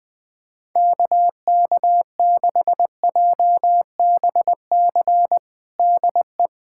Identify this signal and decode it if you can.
Morse code, heard as a keyed tone, about 21 wpm, 700 Hz. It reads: KK6JBC DE
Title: USA Callsign DE Callsign - 6 Characters - 20wpm